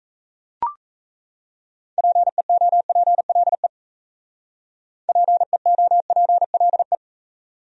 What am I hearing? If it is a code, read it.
Morse code, heard as a keyed tone, about 33 wpm, 700 Hz. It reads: PEOPLE PEOPLE
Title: Speed-Racing Top 200 Words 28wpm